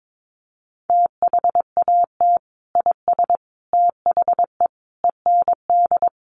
Morse code, heard as a keyed tone, about 22 wpm, 700 Hz. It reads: THAT IS THE END
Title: Sentences from Top 200 Words 22wpm